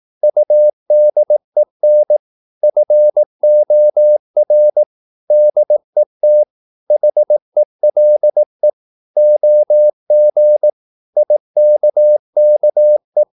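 Morse code, heard as a keyed tone, about 18 wpm, 600 Hz. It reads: UDEN FOR DET HELE OG IKKE
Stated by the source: Slow (QRS) Morse practice at 18 wpm, CW med Gnister.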